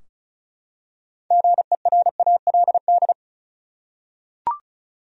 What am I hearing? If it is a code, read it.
Morse code, heard as a keyed tone, about 34 wpm, 700 Hz. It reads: GERALD
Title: USA Common Names - 35wpm